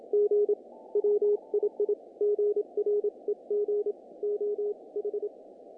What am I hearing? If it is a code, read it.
Morse code, heard as a keyed tone, about 26 wpm, 410 Hz. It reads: G WII GREG OH
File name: contests / naqp-cw-k5tr-op-k5pi- -1800